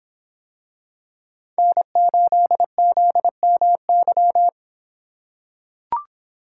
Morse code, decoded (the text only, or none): N8ZMY